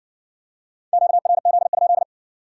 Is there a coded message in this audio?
Yes: Morse code from 0.9 to 2.0 s, heard as a keyed tone, about 60 wpm, 700 Hz. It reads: QRZ?